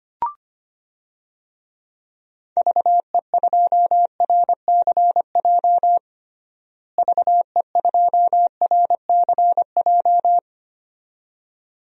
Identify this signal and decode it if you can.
Morse code, heard as a keyed tone, about 25 wpm, 700 Hz. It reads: VE2RCJ VE2RCJ
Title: DXCC-100 Call Signs - 6 Characters - Repeated 2x - 25wpm